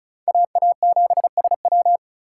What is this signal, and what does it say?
Morse code, heard as a keyed tone, about 36 wpm, 700 Hz. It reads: AA7SW
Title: USA Call Signs - 4x Character Spacing - 35wpm